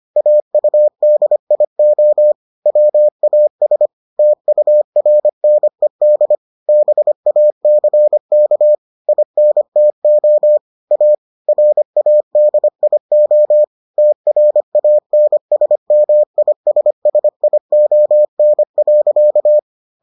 Morse code, heard as a keyed tone, about 25 wpm, 600 Hz. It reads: AUDIO WAS TURNED BACK INTO A RADIO TRANSMISSION.